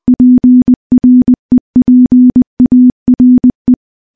Morse code, heard as a keyed tone, about 20 wpm, 250 Hz. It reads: PREPARE
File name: morse-training-20WPM